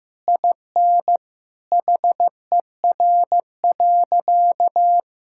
Morse code, heard as a keyed tone, about 14 wpm, 700 Hz. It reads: IN HER.
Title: Sentences from Top 1000 Words - Review Part 5 - 15wpm